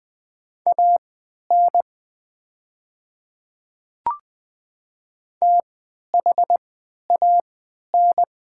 Morse code, heard as a keyed tone, about 20 wpm, spaced as for 11 wpm, 700 Hz. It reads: AN THAN
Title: 3x Character Spacing - Top 200 Words - 20wpm